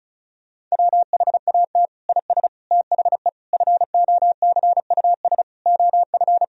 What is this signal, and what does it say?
Morse code, heard as a keyed tone, about 36 wpm, 700 Hz. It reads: WHAT IS THE FOCUS OF
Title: Sentences from Top 1000 Words - Review Part 4 - 35wpm